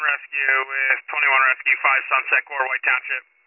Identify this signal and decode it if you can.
Emergency Alert System (SAME) tones, heard as harsh buzzing bursts, with speech. It is heard from 0.4 to 3.0 s.